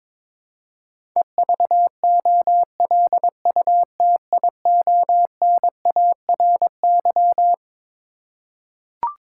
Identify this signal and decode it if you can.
Morse code, heard as a keyed tone, about 22 wpm, 700 Hz. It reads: EVOLUTIONARY
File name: Twelve Letter Words - 22wpm